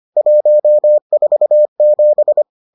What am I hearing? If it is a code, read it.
Morse code, heard as a keyed tone, about 25 wpm, 600 Hz. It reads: 147